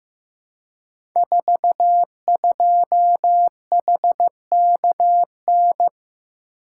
Morse code, heard as a keyed tone, about 15 wpm, 700 Hz. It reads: 42HKN
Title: Code Groups - 5 Letter:Numbers - 15wpm